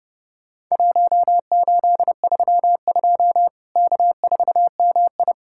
Morse code, heard as a keyed tone, about 30 wpm, 700 Hz. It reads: 1832 K4MI
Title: Worked All Europe DX Contest - QTC Exchange - 30wpm